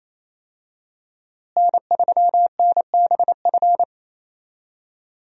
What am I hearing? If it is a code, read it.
Morse code, heard as a keyed tone, about 28 wpm, 700 Hz. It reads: N3NBF
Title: USA Call Signs - 5 Characters - Repeated 2x - 28wpm